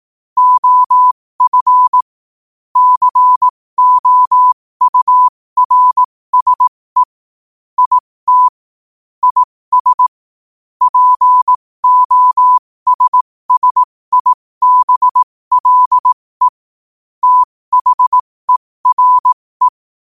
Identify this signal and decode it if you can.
Morse code, heard as a keyed tone, about 18 wpm, spaced as for 15 wpm, 1000 Hz. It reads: OF COURSE IT IS POSSIBLE THERE
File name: qotdfiles / 15wpm